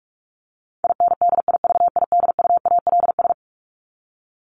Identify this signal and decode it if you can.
Morse code, heard as a keyed tone, about 44 wpm, 700 Hz. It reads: INDIVIDUALS
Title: ES:S Suffix - Top 100 Words with Base Words 45wpm